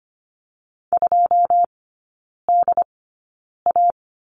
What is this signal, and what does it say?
Morse code, heard as a keyed tone, about 25 wpm, spaced as for 9 wpm, 690 Hz. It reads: 2DA
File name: Lesson 14e - Number 2 - Three Characters 25wpm